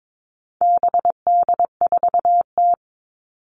Morse code, heard as a keyed tone, about 22 wpm, 700 Hz. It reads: BD4T
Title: All Cut Numbers - 4-Digit 22wpm